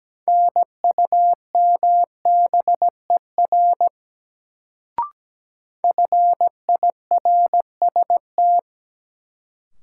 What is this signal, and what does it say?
Morse code, heard as a keyed tone, about 17 wpm, 700 Hz. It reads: NUMBER FIRST
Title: Sentences from Top 100 Words - New Word Drill - 17wpm